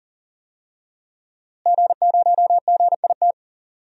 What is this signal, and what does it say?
Morse code, heard as a keyed tone, about 39 wpm, 700 Hz. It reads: G0GIT
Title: DXCC-100 Call Signs - 5 Characters - Repeated 2x - 40wpm